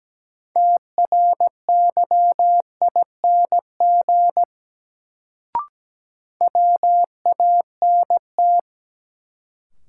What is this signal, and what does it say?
Morse code, heard as a keyed tone, about 17 wpm, 700 Hz. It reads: TRYING WANT
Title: Sets of 2 Words - New Word Drill - 17wpm